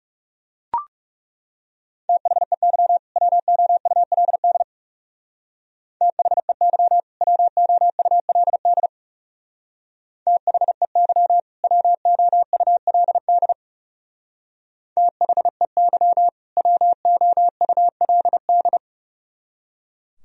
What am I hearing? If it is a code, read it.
Morse code, heard as a keyed tone, about 38 wpm, 700 Hz. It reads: THEY WOULD THEY WOULD THEY WOULD THEY WOULD